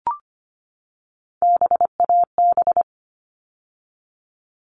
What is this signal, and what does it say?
Morse code, heard as a keyed tone, about 25 wpm, 700 Hz. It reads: BAB
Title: All Cut Numbers - 3-Digit 25wpm